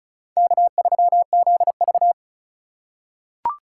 Morse code, heard as a keyed tone, about 36 wpm, 700 Hz. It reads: K3ZV
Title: USA Call Signs - 4 Characters - 4x Character Spacing - 35wpm